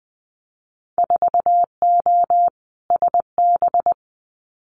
Morse code, heard as a keyed tone, about 20 wpm, 700 Hz. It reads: 4O SB
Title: Winter Field Day Exchanges 20wpm